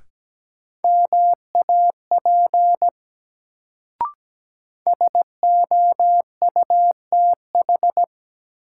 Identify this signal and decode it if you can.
Morse code, heard as a keyed tone, about 17 wpm, 700 Hz. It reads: MAP SOUTH
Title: Sentences from Top 400 Words - New Word Drill - 17wpm